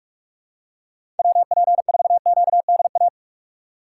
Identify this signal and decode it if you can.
Morse code, heard as a keyed tone, about 46 wpm, 690 Hz. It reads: WP4QDA
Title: USA Call Signs - 6 Characters - 3x Character Spacing - 45wpm